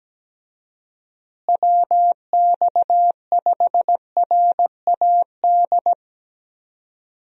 Morse code, heard as a keyed tone, about 17 wpm, 700 Hz. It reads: WX5RAD
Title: USA Call Signs - 6 Characters - Repeated 3x - 17wpm